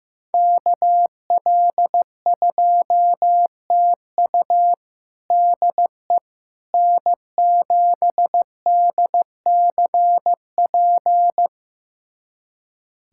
Morse code, heard as a keyed tone, about 15 wpm, 700 Hz. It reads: KL2TU DE N7DCP
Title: USA Callsign DE Callsign - 5 Characters - 15wpm